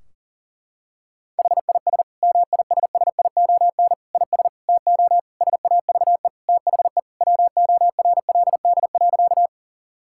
Morse code, heard as a keyed tone, about 40 wpm, 700 Hz. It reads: HIS MISSION IS TO SAVE THE WORLD.
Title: Sentences from Top 1600 Words 40wpm